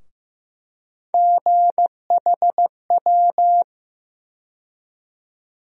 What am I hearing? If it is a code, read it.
Morse code, heard as a keyed tone, about 15 wpm, 700 Hz. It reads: GHW
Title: USA Call Sign Postfixes - 3 Letters 15wpm